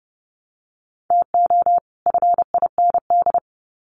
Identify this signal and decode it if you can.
Morse code, heard as a keyed tone, about 30 wpm, 700 Hz. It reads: TO FIND
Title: Sets of 2 Words 30wpm